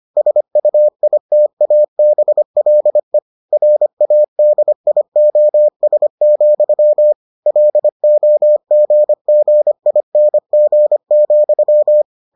Morse code, heard as a keyed tone, about 25 wpm, 600 Hz. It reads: SUITABLE RADIOS, LOGGING,